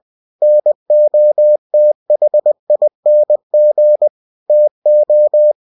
Morse code, heard as a keyed tone, about 20 wpm, 600 Hz. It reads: NOTHING TO
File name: cw-books / winnie-the-pooh / winnie-20wpm-page-0022